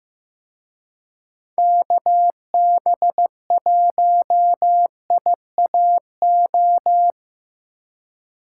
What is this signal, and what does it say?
Morse code, heard as a keyed tone, about 18 wpm, 700 Hz. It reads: KB1IAO
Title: Speed Racing - USA 2x3 Call Signs 15wpm